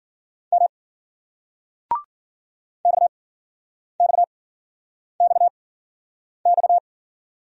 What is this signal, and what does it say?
Morse code, heard as a keyed tone, about 49 wpm, spaced as for 9 wpm, 700 Hz. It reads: M XXXX
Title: Speed-Racing Single Letters 40wpm